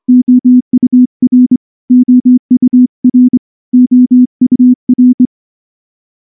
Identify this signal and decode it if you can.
Morse code, heard as a keyed tone, about 23 wpm, 250 Hz. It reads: OUR OUR OUR
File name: morse / morse-training-25WPM